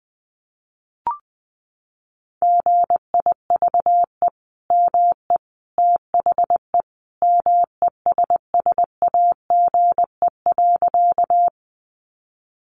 Morse code, heard as a keyed tone, about 20 wpm, 700 Hz. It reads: GIVE ME THE MESSAGE.
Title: Sentences from Top 1100 Words 20wpm